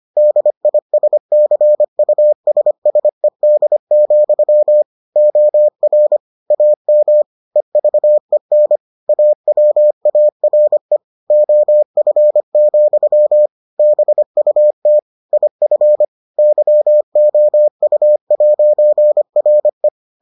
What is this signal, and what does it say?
Morse code, heard as a keyed tone, about 25 wpm, 600 Hz. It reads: DISCUSSED, OR AM EVEN AWARE OF, BUT IF YOU'RE